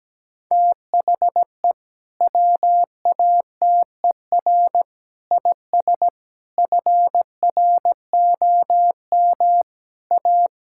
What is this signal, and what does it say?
Morse code, heard as a keyed tone, about 17 wpm, 700 Hz. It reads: THE WATER IS FROM A
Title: Sentences from Top 1000 Words - Review Part 5 - 17wpm